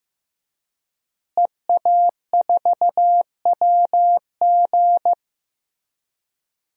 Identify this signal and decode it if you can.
Morse code, heard as a keyed tone, about 15 wpm, 700 Hz. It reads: EA4WG
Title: DXCC-100 Call Signs - 5 Characters - Repeated 3x - 15wpm